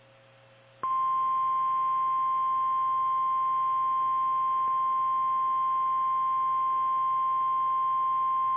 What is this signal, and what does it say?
Emergency Alert System (SAME) tones (the 1050 Hz attention tone). It is heard from 0.8 to 7.7 s.
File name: eas / KKPRKXPN / KKPRKXPN